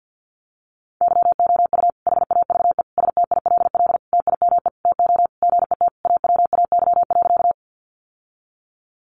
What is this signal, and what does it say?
Morse code, heard as a keyed tone, about 50 wpm, 700 Hz. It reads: YOU HAVE STILL TIME TO GET AWAY.
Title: Sentences from Top 600 Words 50wpm